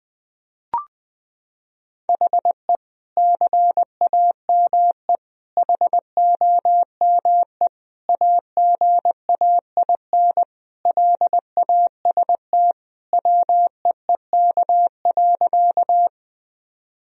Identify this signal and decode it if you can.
Morse code, heard as a keyed tone, about 20 wpm, 700 Hz. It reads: HE CAME HOME AGAIN LAST WEEK.
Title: Sentences from Top 1000 Words - Review Part 2 - 20wpm